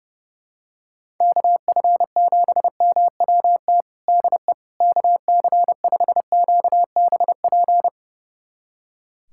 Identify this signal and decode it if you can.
Morse code, heard as a keyed tone, about 30 wpm, 700 Hz. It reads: KF7MWT DE KC5QBP
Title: USA Callsign DE Callsign - 6 Characters - 30wpm